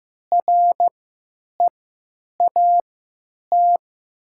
Morse code, heard as a keyed tone, about 15 wpm, spaced as for 9 wpm, 700 Hz. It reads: REAT
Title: 3x Character Spacing - Top 500 Words - 15wpm